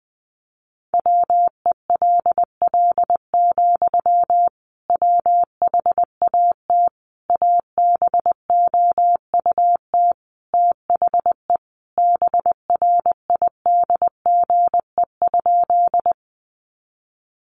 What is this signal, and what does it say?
Morse code, heard as a keyed tone, about 20 wpm, 700 Hz. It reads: WELL, WHAT ABOUT THE BRIDGE?
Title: Sentences from Top 1200 Words 20wpm